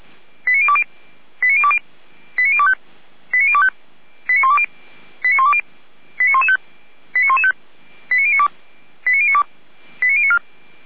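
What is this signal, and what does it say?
Five-tone selective call (ZVEI), heard as quick runs of five tones, tones of about 70 ms. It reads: 89020 89020 89025 89025 89120 89120 89115 89115 89092 89092 89094
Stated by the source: ZVEI five-tone call 89270 received on SDR.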